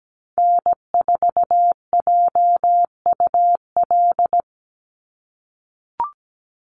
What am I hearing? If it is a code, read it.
Morse code, heard as a keyed tone, about 17 wpm, 700 Hz. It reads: N4JUL
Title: USA Call Signs 17wpm